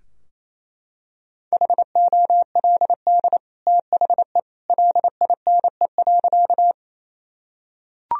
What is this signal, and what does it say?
Morse code, heard as a keyed tone, about 28 wpm, 700 Hz. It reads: HOLD THE LINE.
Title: Sentences from Top 1000 Words - Review Part 2 - 28wpm